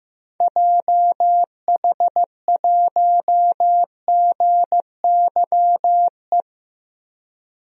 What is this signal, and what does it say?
Morse code, heard as a keyed tone, about 15 wpm, 700 Hz. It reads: JH1GYE
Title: DXCC-100 Call Signs - 6 Characters - 15wpm